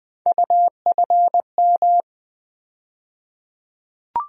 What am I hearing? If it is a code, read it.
Morse code, heard as a keyed tone, about 20 wpm, 700 Hz. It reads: UFM
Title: USA Call Sign Postfixes - 3 Letters 20wpm